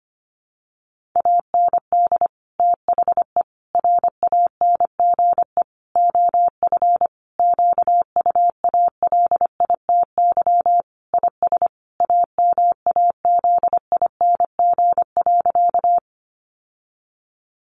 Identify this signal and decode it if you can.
Morse code, heard as a keyed tone, about 25 wpm, 700 Hz. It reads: AND THE RANGE OF QUALITY IS AMAZING.
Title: Sentences from Top 1500 Words 25wpm